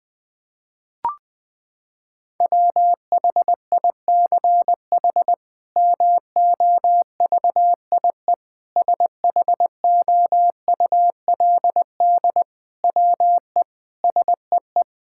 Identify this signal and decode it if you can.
Morse code, heard as a keyed tone, about 21 wpm, 700 Hz. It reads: WHICH MOVIE SHOULD WE SEE
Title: Sentences from Top 1000 Words - Review Part 5 - 20wpm